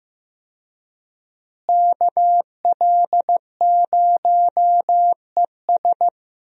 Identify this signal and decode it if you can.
Morse code, heard as a keyed tone, about 15 wpm, 700 Hz. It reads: KL0ES
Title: USA Call Signs - 5 Characters - Repeated 2x - 15wpm